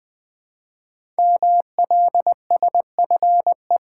Morse code, heard as a keyed tone, about 20 wpm, 700 Hz. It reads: MLSFE
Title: Code Groups - 5 Letters - 20wpm